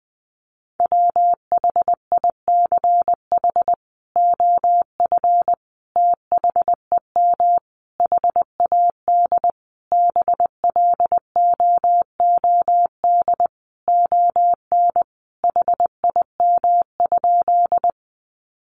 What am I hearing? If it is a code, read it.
Morse code, heard as a keyed tone, about 20 wpm, 700 Hz. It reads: WHICH OF THEM HAD BLOOD ON HIM?
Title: Sentences from Top 1100 Words 20wpm